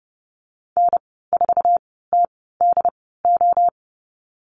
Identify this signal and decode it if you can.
Morse code, heard as a keyed tone, about 30 wpm, spaced as for 17 wpm, 700 Hz. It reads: N4TDO
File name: Lesson 14g - Number 2 - Call Signs 30wpm